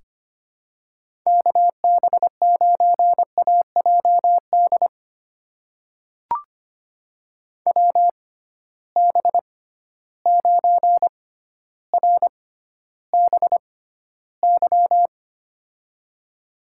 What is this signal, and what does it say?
Morse code, heard as a keyed tone, about 25 wpm, spaced as for 13 wpm, 700 Hz. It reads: KB9AJD WB9RBY
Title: USA Call Signs - 6 Characters - 6x Character Spacing - 25wpm